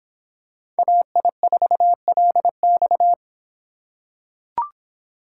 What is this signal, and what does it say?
Morse code, heard as a keyed tone, about 24 wpm, 700 Hz. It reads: AI4LX